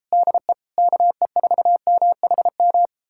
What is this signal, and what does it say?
Morse code, heard as a keyed tone, about 37 wpm, 700 Hz. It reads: DE KE4MHM